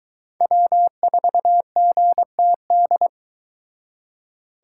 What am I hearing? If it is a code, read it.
Morse code, heard as a keyed tone, about 22 wpm, 700 Hz. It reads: W4GTD